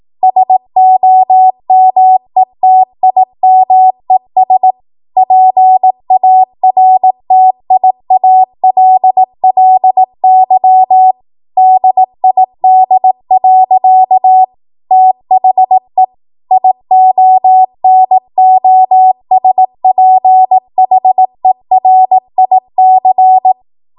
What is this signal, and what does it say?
Morse code, heard as a keyed tone, about 18 wpm, 750 Hz. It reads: SOMETIMES PARTIALLY DID. THE IONOSPHERIC
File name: Morse / 18WPM